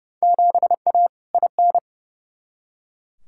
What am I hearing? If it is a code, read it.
Morse code, heard as a keyed tone, about 31 wpm, 700 Hz. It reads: 7A IN
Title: Field Day Exchanges Encore - 30wpm